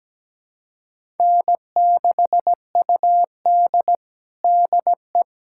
Morse code, heard as a keyed tone, about 17 wpm, 700 Hz. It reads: N6UD DE
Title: USA Callsign DE Callsign - 4 Characters - 17wpm